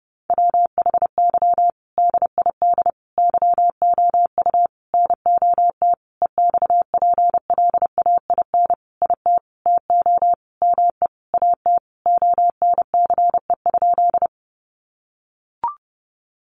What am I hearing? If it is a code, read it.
Morse code, heard as a keyed tone, about 30 wpm, 700 Hz. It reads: WHY DID YOU NOT EXPLAIN IT TO ME AT ONCE?
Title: Sentences from Top 1100 Words 30wpm